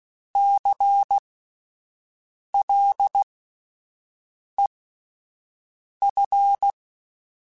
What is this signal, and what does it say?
Morse code, heard as a keyed tone, about 16 wpm, spaced as for 5 wpm, 800 Hz. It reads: CLEF
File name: Spaced Alphas / Spaced Alphas mp3s / Cs sp @10